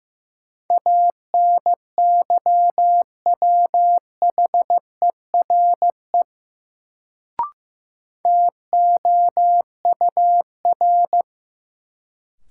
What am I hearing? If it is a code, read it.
Morse code, heard as a keyed tone, about 16 wpm, 700 Hz. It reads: ANYWHERE TOUR
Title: Sentences from Top 1400 Words - New Word Drill Alt - 15wpm